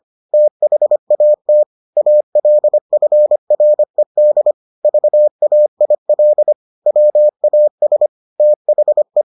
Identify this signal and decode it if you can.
Morse code, heard as a keyed tone, about 25 wpm, 600 Hz. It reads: THAT ALFRED VAIL WAS THE